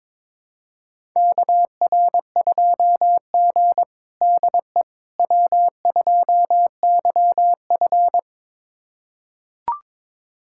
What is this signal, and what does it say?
Morse code, heard as a keyed tone, about 22 wpm, 700 Hz. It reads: KR2G DE W2YF
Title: USA Callsign DE Callsign - 4 Characters - 22wpm